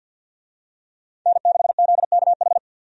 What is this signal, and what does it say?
Morse code, heard as a keyed tone, about 50 wpm, 700 Hz. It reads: N6ZKH
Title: USA Call Signs - 5 Characters - 50wpm